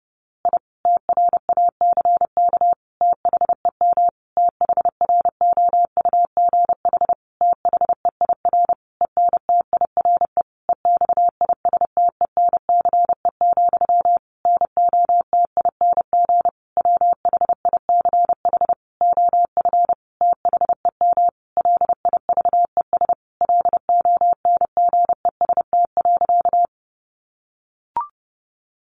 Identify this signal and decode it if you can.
Morse code, heard as a keyed tone, about 30 wpm, 700 Hz. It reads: I TRACK THEM THROUGH THEIR ENTIRE EXISTENCE, NOTING WHICH OF THEM LIVES LONGEST.
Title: A Sound Of Thunder 30wpm